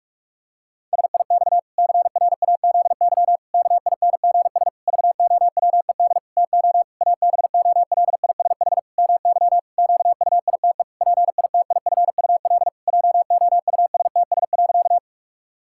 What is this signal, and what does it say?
Morse code, heard as a keyed tone, about 46 wpm, 700 Hz. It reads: SIX CRAZY KINGS VOWED TO ABOLISH MY QUITE PITIFUL JOUSTS.
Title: Pangram Challenge v2 - 45wpm